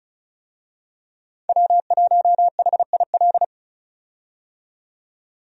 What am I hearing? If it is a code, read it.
Morse code, heard as a keyed tone, about 36 wpm, 700 Hz. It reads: W1HIL
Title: USA Call Signs - 5 Characters - 35wpm